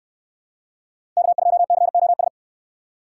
Morse code, heard as a keyed tone, about 68 wpm, 700 Hz. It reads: K2COS